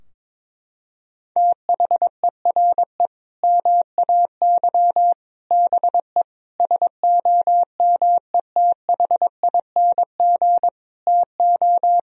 Morse code, heard as a keyed tone, about 22 wpm, 700 Hz. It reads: THERE MAY BE SOMETHING TO
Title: Sentences from Top 1000 Words - Review Part 3 - 22wpm